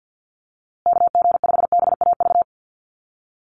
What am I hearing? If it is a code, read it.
Morse code, heard as a keyed tone, about 50 wpm, 700 Hz. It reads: KG5BAV
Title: USA Call Signs Repeated 2x 50wpm